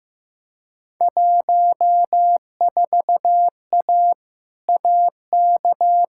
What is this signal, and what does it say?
Morse code, heard as a keyed tone, about 15 wpm, 700 Hz. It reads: 14A AK
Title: Field Day Exchanges Encore - 15wpm